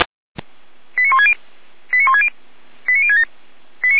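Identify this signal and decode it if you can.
Five-tone selective call (ZVEI), heard as quick runs of five tones, tones of about 70 ms. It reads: 89160 89160 89067
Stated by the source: ZVEI five-tone call 89072 received on an SDR.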